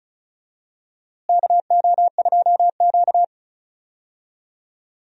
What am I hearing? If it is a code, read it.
Morse code, heard as a keyed tone, about 36 wpm, 700 Hz. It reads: KO2Q
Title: USA Call Signs - 4 Characters - Repeated 2x - 35wpm